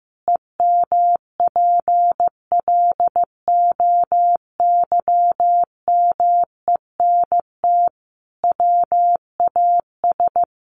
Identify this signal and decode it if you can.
Morse code, heard as a keyed tone, about 15 wpm, 700 Hz. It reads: EMPLOYMENT WAS
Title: Sentences from Top 1500 Words 15wpm